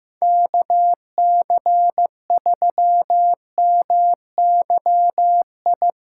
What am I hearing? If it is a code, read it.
Morse code, heard as a keyed tone, about 15 wpm, 700 Hz. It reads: KC3MYI
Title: USA Callsign DE Callsign - 6 Characters - 15wpm